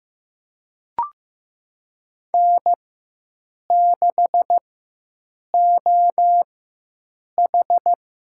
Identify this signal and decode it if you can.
Morse code, heard as a keyed tone, about 16 wpm, spaced as for 7 wpm, 700 Hz. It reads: N6OH
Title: USA Call Signs - 4 Characters - 4x Character Spacing - 15wpm